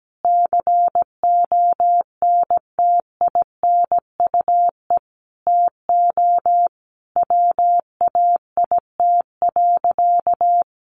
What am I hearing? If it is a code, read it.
Morse code, heard as a keyed tone, about 17 wpm, 700 Hz. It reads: CONTINUE TO WAIT.
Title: Easy Phrases 17wpm